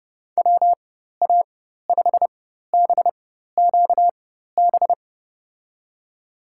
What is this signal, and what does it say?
Morse code, heard as a keyed tone, about 30 wpm, spaced as for 14 wpm, 700 Hz. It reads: WA5BQB
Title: USA Call Signs - 6 Characters - 4x Character Spacing - 30wpm